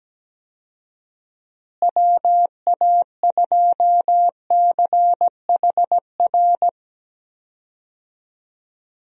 Morse code, heard as a keyed tone, about 17 wpm, 700 Hz. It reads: WA2CHR
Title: USA Call Signs - 6 Characters - Repeated 2x - 17wpm